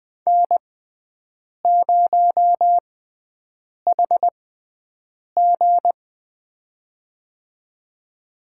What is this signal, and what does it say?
Morse code, heard as a keyed tone, about 20 wpm, spaced as for 7 wpm, 700 Hz. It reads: N0HG
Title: USA Call Signs - 4 Characters - 6x Character Spacing - 20wpm